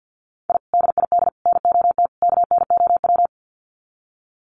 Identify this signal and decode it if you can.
Morse code, heard as a keyed tone, about 48 wpm, 700 Hz. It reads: I DID NOT KNOW
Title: Sets of 4 Words 50wpm